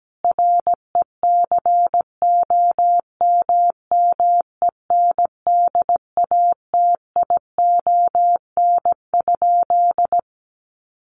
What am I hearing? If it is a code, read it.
Morse code, heard as a keyed tone, about 17 wpm, 700 Hz. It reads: RECOMMENDATION?
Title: Sentences from Top 1400 Words 17wpm